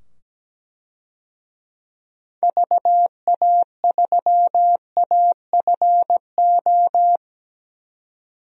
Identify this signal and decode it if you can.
Morse code, heard as a keyed tone, about 17 wpm, 700 Hz. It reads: VA3AFO
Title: DXCC-100 Call Signs - 6 Characters - 17wpm